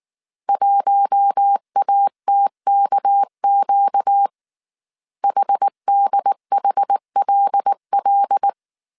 Morse code, heard as a keyed tone, about 19 wpm, 800 Hz. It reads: 1ATKQ HDHLL
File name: CW / DET / 1 6 1500C S A 18wpm 2255